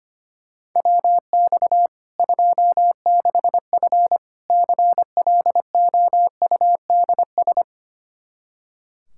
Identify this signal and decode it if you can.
Morse code, heard as a keyed tone, about 25 wpm, 700 Hz. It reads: WX 26F CLOUDS
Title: Practice QSO - Exchange 2 - Info - 25wpm